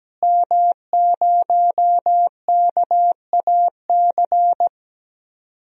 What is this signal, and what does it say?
Morse code, heard as a keyed tone, about 17 wpm, 700 Hz. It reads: M0KAC
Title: DXCC-100 Call Signs - 5 Characters - Repeated 3x - 17wpm